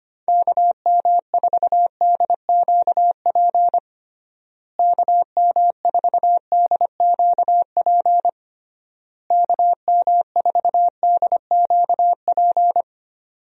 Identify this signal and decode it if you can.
Morse code, heard as a keyed tone, about 25 wpm, 700 Hz. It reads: KM4DQP KM4DQP KM4DQP
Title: USA Call Signs - 6 Characters - Repeated 3x - 25wpm